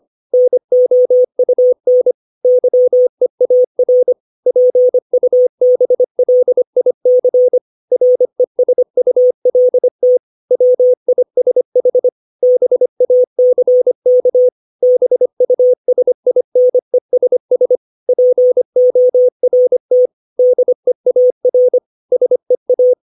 Morse code, heard as a keyed tone, about 25 wpm, 500 Hz. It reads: NOUN YEAR PUBLIC RESULT WISH BACK BUSINESS PORT DEAR SEA
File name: 632 words / 632 Words 25WPM